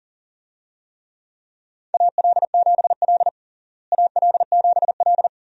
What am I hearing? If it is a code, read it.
Morse code, heard as a keyed tone, about 40 wpm, 700 Hz. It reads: AL7L AL7L
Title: USA Call Signs - 4 Characters - Repeated 3x - 40wpm